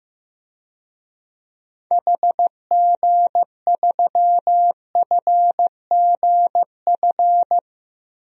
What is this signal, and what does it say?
Morse code, heard as a keyed tone, about 15 wpm, 700 Hz. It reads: HG3FGF
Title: DXCC-100 Call Signs - 6 Characters - Repeated 3x - 15wpm